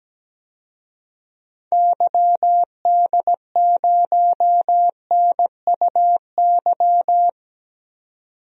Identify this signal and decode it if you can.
Morse code, heard as a keyed tone, about 17 wpm, 700 Hz. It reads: YD0NUY
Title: DXCC-100 Call Signs - 6 Characters - 17wpm